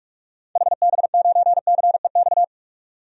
Morse code, heard as a keyed tone, about 44 wpm, 700 Hz. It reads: HB9CEX
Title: DXCC-100 Call Signs - 6 Characters - 45wpm